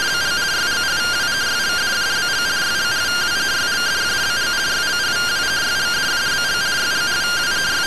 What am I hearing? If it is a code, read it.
Two-tone teleprinter signal, about 170 Hz shift.